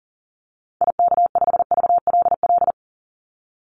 Morse code, heard as a keyed tone, about 37 wpm, 700 Hz. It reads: IK5VLL
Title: Call Signs Repeated 2x - 40wpm